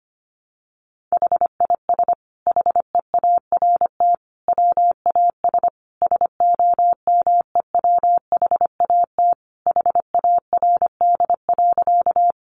Morse code, heard as a keyed tone, about 25 wpm, 700 Hz. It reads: HIS HEART WAS SOMEWHAT HARD.
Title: Sentences from Top 1300 Words 25wpm